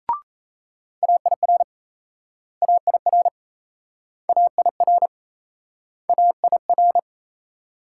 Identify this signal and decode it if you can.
Morse code, heard as a keyed tone, about 36 wpm, 700 Hz. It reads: AIR AIR AIR AIR